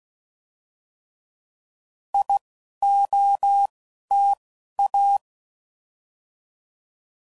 Morse code, heard as a keyed tone, about 16 wpm, 800 Hz. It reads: I O T A
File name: Prefixes / By First Letter / Is